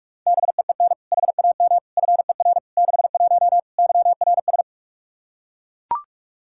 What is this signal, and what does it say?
Morse code, heard as a keyed tone, about 41 wpm, 700 Hz. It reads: BEEN HAM FER 61 YRS
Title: K1USN SST QSO Practice 45wpm